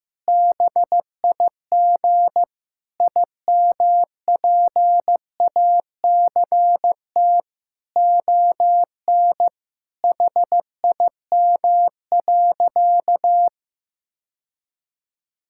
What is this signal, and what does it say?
Morse code, heard as a keyed tone, about 15 wpm, 700 Hz. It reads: BIG IMPACT ON HIM.
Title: Sentences from Top 1000 Words - Review Part 6 - 15wpm